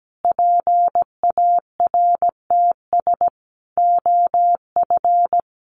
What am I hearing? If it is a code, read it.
Morse code, heard as a keyed tone, about 17 wpm, 700 Hz. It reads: PARTS OF
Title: Sets of 5 Words Encore 17wpm